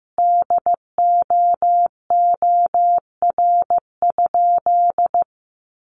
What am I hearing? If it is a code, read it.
Morse code, heard as a keyed tone, about 15 wpm, 700 Hz. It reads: DOOR?
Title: Sentences from Top 1300 Words 15wpm